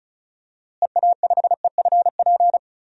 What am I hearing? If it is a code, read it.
Morse code, heard as a keyed tone, about 34 wpm, 700 Hz. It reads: EA5EFP
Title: DXCC-100 Call Signs - 6 Characters - Repeated 2x - 35wpm